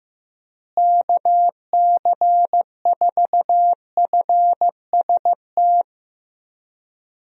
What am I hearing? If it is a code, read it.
Morse code, heard as a keyed tone, about 15 wpm, 700 Hz. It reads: KC4FST
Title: USA Call Signs - 6 Characters - Repeated 2x - 15wpm